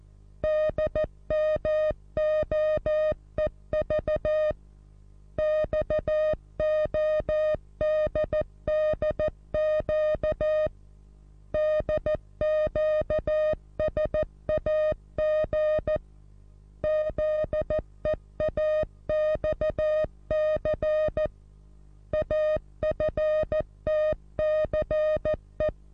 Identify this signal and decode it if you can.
Morse code, heard as a keyed tone, about 14 wpm, 620 Hz. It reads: DMOEV XODDQ DQSAG ZEAXC AFTCE
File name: repos / Morsekurs / DJ4UF Morsen27